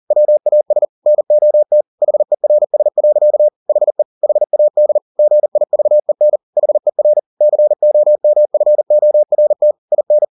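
Morse code, heard as a keyed tone, about 40 wpm, 600 Hz. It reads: WAS NOT HERS. HE HAD GIVEN HER COMFORT IN
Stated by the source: Morse code at 40 wpm, practice text, CW med Gnister.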